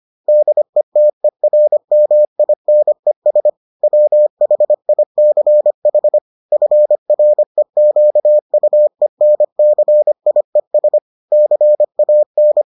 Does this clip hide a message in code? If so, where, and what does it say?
Yes: Morse code from 0.3 to 12.6 s, heard as a keyed tone, about 25 wpm, 600 Hz. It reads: DETERMINES WHICH FREQUENCIES CAN